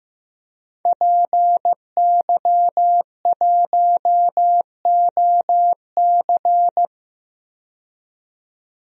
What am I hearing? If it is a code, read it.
Morse code, heard as a keyed tone, about 15 wpm, 700 Hz. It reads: PY1OC
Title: DXCC-100 Call Signs - 5 Characters - 15wpm